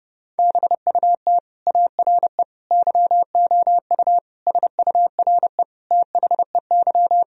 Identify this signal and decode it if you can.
Morse code, heard as a keyed tone, about 31 wpm, 700 Hz. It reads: BUT ARE YOU SURE THEY
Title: Sentences from Top 1000 Words - Review Part 1 - 30wpm